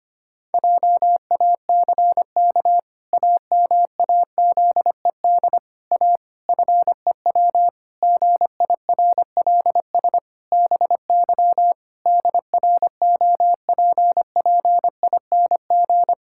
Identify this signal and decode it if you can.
Morse code, heard as a keyed tone, about 25 wpm, 700 Hz. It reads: JACK AMAZED A FEW GIRLS BY DROPPING
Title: Pangram Challenge v2 - 25wpm